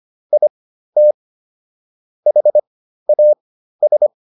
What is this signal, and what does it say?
Morse code, heard as a keyed tone, about 25 wpm, spaced as for 13 wpm, 600 Hz. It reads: IT HAS